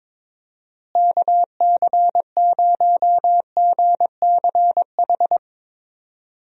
Morse code, heard as a keyed tone, about 22 wpm, 700 Hz. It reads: KC0GCH
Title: USA Call Signs - 6 Characters - Repeated 3x - 22wpm